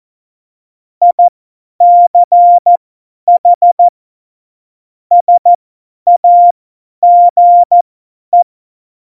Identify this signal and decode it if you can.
Morse code, heard as a keyed tone, about 14 wpm, spaced as for 10 wpm, 700 Hz. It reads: ICH SAGE
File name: Morsekurs / Witze / Witz 25. Dezember, 50 F70